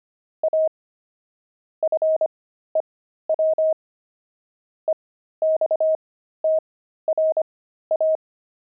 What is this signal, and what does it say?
Morse code, heard as a keyed tone, about 25 wpm, spaced as for 13 wpm, 630 Hz. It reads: A FEW EXTRA
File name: pangram-fs-13-cs-25-pavement